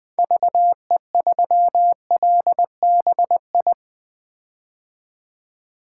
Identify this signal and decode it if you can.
Morse code, heard as a keyed tone, about 21 wpm, 700 Hz. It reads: VE3LBI
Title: 2023 POTA Hunter Call Signs - 20wpm